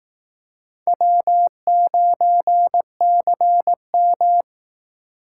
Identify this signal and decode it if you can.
Morse code, heard as a keyed tone, about 18 wpm, 700 Hz. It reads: W9CM
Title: Speed Racing - USA 1x2 Call Signs 15wpm